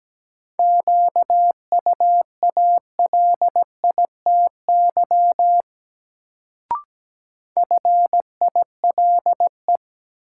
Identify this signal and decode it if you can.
Morse code, heard as a keyed tone, about 17 wpm, 700 Hz. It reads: QUALITY FILE
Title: Sentences from Top 1000 Words - Review Part 4 - New Word Drill - 17wpm